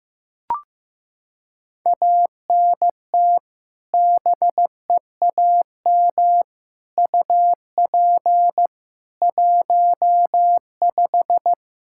Morse code, heard as a keyed tone, about 15 wpm, 700 Hz. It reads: ANT BEAM UP 15
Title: Practice QSO - Exchange 2 - Info - 15wpm